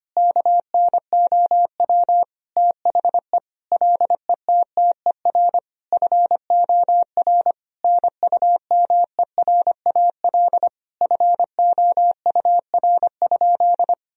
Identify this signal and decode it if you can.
Morse code, heard as a keyed tone, about 25 wpm, 700 Hz. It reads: KNOW THE LETTER FOR NUMERAL FOUR?
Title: Sentences from Top 1000 Words - Review Part 2 - 25wpm